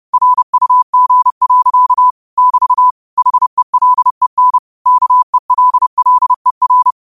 Morse code, heard as a keyed tone, about 30 wpm, 1000 Hz. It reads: RUG. <BT> HELEN KELLER